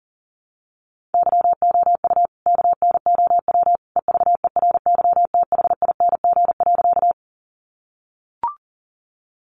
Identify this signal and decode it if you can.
Morse code, heard as a keyed tone, about 37 wpm, 700 Hz. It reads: YOU KNOW EVERYTHING.
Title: Sentences from Top 700 Words 40wpm